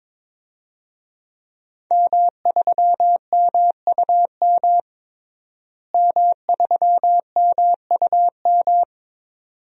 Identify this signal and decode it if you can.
Morse code, heard as a keyed tone, about 21 wpm, 700 Hz. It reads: M3MUM M3MUM
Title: DXCC-100 Call Signs - 5 Characters - Repeated 2x - 22wpm